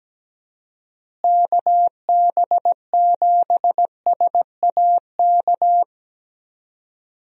Morse code, heard as a keyed tone, about 17 wpm, 700 Hz. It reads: KB7SAK
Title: USA Call Signs - 6 Characters - Repeated 2x - 17wpm